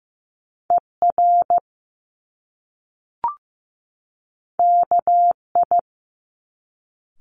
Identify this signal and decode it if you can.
Morse code, heard as a keyed tone, about 15 wpm, 700 Hz. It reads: ER KI
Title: Top Two Letter Combinations in Words 15wpm